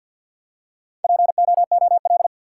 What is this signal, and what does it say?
Morse code, heard as a keyed tone, about 50 wpm, 700 Hz. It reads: POOL
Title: Four Letter Words - 50wpm